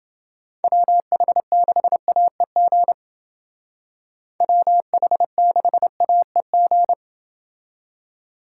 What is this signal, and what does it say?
Morse code, heard as a keyed tone, about 29 wpm, 700 Hz. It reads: WH6AEG WH6AEG